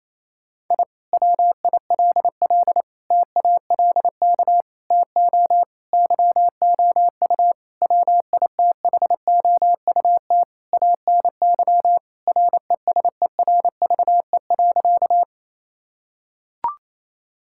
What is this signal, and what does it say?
Morse code, heard as a keyed tone, about 28 wpm, 700 Hz. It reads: I WILL TALK TO YOU WITHOUT ANY RESERVE.
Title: Sentences from Top 1000 Words - Review Part 6 - 28wpm